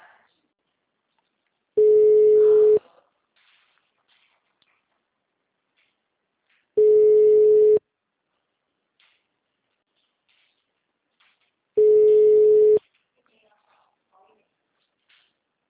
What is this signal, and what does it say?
Ringback tone (Europe, 425 Hz).